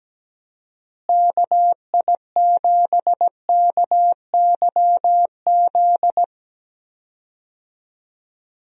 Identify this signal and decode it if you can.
Morse code, heard as a keyed tone, about 17 wpm, 700 Hz. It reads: KI7KYZ
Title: USA Call Signs - 6 Characters - Repeated 2x - 17wpm